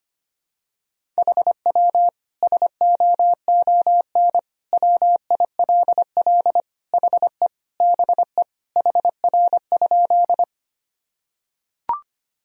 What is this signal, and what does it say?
Morse code, heard as a keyed tone, about 25 wpm, 700 Hz. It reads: HW SOON WILL HE BE HR?
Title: Sentences from Top 400 Words and Abbreviations 25wpm